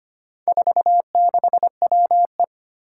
Morse code, heard as a keyed tone, about 25 wpm, 700 Hz. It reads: 46WE
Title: Four Challenging Letter-Number Combinations 25wpm